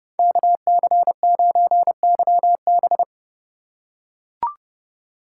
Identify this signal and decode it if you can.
Morse code, heard as a keyed tone, about 29 wpm, 700 Hz. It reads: KC9YB